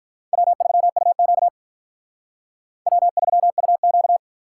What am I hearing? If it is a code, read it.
Morse code, heard as a keyed tone, about 44 wpm, 700 Hz. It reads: W3UQ W3UQ